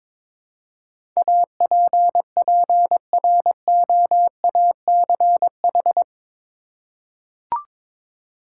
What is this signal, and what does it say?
Morse code, heard as a keyed tone, about 22 wpm, 700 Hz. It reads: APPROACH
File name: practice / 22wpm / Eight Letter Words - 22wpm